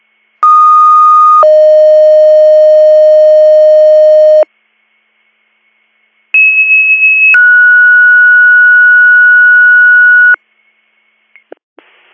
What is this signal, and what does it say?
Two-tone page, heard as a one-second tone, then a three-second tone, about 1230 then 620 Hz, about 2600 then 1460 Hz. It is heard from 0.4 to 10.4 s.